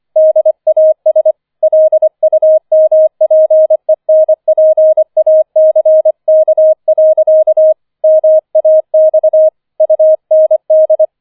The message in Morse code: DAS LUMPENPACK. MAX UND

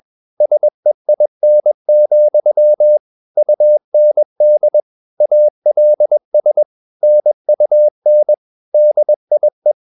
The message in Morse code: SEIN, UND ALS NUN DIE

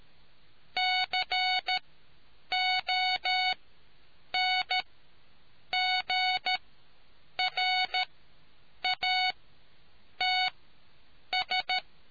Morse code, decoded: CONGRATS

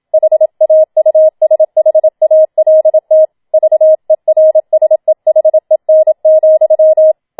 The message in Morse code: HAUSHALT VERSEHEN,

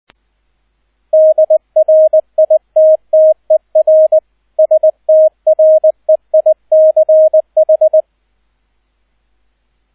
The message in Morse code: DRITTER STREICH

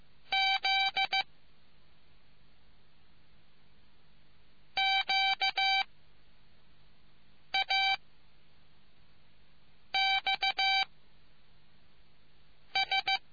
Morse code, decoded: Z Q A X S

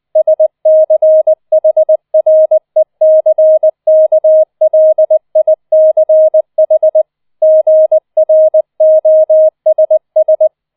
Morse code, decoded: SCHRECKLICH GROSS

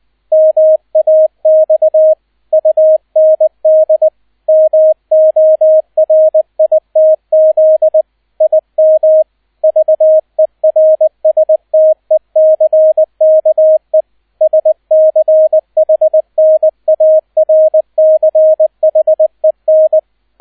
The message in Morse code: MAX UND MORITZ IM VERSTECKE SCHNARCHEN

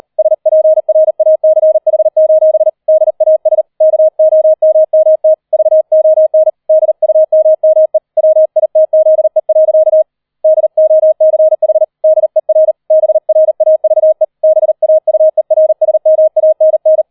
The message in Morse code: SPRACH: DAS KOMMT VON DUMME WITZE. DOCH DER BRAVE BAUERSMANN

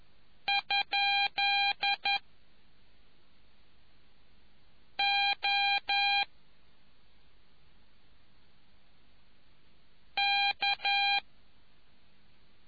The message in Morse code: ? O K

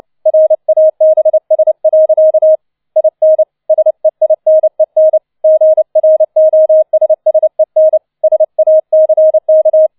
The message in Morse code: RABS. IN SEINEN GROSSEN SACK

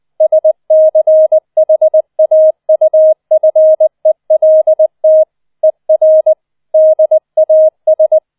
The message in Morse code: SCHAUFELT ER DAS

